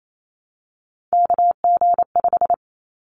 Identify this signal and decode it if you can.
Morse code, heard as a keyed tone, about 28 wpm, 700 Hz. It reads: KG5
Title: Callsign Prefixes for Top 25 DXCC - 28wpm